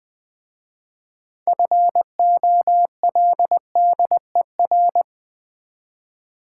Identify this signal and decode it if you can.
Morse code, heard as a keyed tone, about 20 wpm, 690 Hz. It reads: FOLDER
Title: Difficult Words with Hints - 20wpm